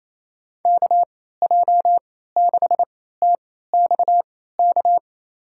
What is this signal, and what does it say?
Morse code, heard as a keyed tone, about 28 wpm, spaced as for 16 wpm, 700 Hz. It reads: KJ6TXK
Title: USA Call Signs - 6 Characters - 3x Character Spacing - 28wpm